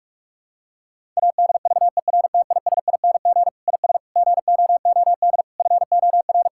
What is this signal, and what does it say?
Morse code, heard as a keyed tone, about 41 wpm, 700 Hz. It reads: ADVERTISING IS GOOD FOR
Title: Sentences from Top 1000 Words - Review Part 6 - 45wpm